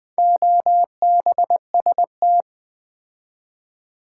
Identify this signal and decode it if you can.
Morse code, heard as a keyed tone, about 20 wpm, 700 Hz. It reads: OBST
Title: Build A Word - 4 Letters At A Time - 8-Letter Words - 20wpm